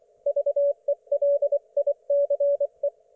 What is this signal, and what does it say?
Morse code, heard as a keyed tone, about 24 wpm, 570 Hz. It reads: VELICE
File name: Archiv / CUC224 / ok1fhi cul224